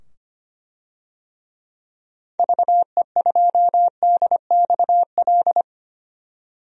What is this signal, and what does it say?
Morse code, heard as a keyed tone, about 25 wpm, 700 Hz. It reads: VE2DXL
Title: DXCC-100 Call Signs - 6 Characters - 25wpm